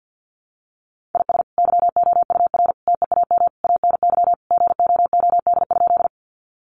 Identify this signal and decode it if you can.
Morse code, heard as a keyed tone, about 50 wpm, 700 Hz. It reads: IS YOUR TEAM ANY GOOD?
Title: Sentences from Top 1500 Words 50wpm